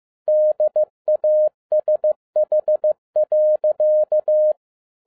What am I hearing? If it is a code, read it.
Morse code, heard as a keyed tone, about 15 wpm, 610 Hz. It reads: DASH.